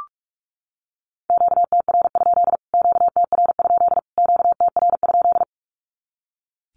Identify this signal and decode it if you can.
Morse code, heard as a keyed tone, about 44 wpm, 700 Hz. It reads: QTR? QTR? QTR?
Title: Q-Codes Repeated 3x 45wpm